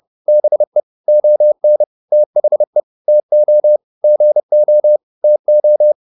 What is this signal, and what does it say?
Morse code, heard as a keyed tone, about 30 wpm, 600 Hz. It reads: BE ON THE TO GO TO